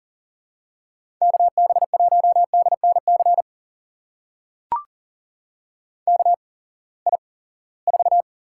Morse code, heard as a keyed tone, about 40 wpm, spaced as for 23 wpm, 700 Hz. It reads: KB1DNC KI4
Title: USA Call Signs - 6 Characters - 8x Character Spacing - 40wpm